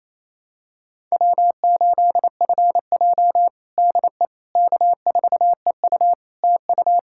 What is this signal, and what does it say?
Morse code, heard as a keyed tone, about 28 wpm, 700 Hz. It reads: W8FJ DE K4EU TU
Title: Practice QSO - Exchange 4 - Ending - 28wpm